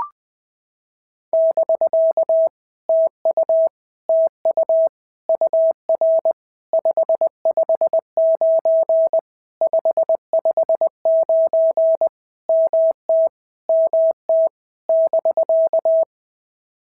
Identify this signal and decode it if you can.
Morse code, heard as a keyed tone, about 20 wpm, 650 Hz. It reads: <BK> TU TU UR 559 559 MT MT <BK>
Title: POTA QSO Practice - 20wpm